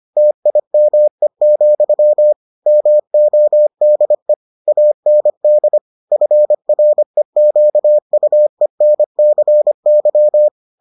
Morse code, heard as a keyed tone, about 25 wpm, 600 Hz. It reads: TIME, MODE AND FREQUENCY